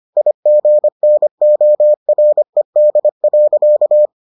Morse code, heard as a keyed tone, about 25 wpm, 600 Hz. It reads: IGNORED.